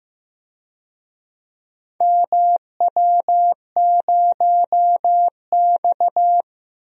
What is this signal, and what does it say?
Morse code, heard as a keyed tone, about 15 wpm, 700 Hz. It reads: MW0X
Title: DXCC-100 Call Signs - 4 Characters - Repeated 2x - 15wpm